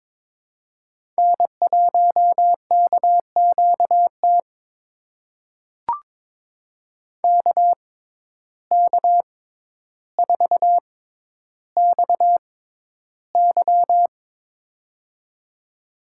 Morse code, heard as a keyed tone, about 22 wpm, spaced as for 11 wpm, 700 Hz. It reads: N1KQT KK4XY
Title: USA Call Signs - 5 Characters - 6x Character Spacing - 22wpm